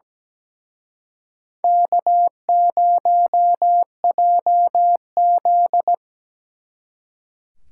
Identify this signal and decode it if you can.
Morse code, heard as a keyed tone, about 20 wpm, 700 Hz. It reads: K0JZ
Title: Speed Racing - USA Call Signs 17wpm